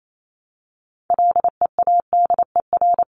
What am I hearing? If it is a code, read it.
Morse code, heard as a keyed tone, about 28 wpm, 700 Hz. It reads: LEADER
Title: Top 1501-1750 Words 28wpm